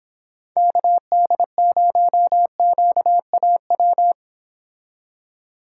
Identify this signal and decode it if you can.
Morse code, heard as a keyed tone, about 26 wpm, 700 Hz. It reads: KD0QAW